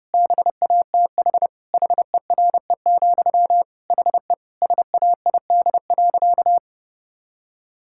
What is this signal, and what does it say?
Morse code, heard as a keyed tone, about 30 wpm, 700 Hz. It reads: BATH HERE, HE SAID.
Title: Sentences from Top 1600 Words 30wpm